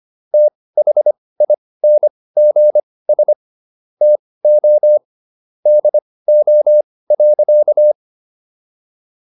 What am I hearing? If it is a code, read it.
Morse code, heard as a keyed tone, about 25 wpm, spaced as for 18 wpm, 600 Hz. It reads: THINGS TO DO.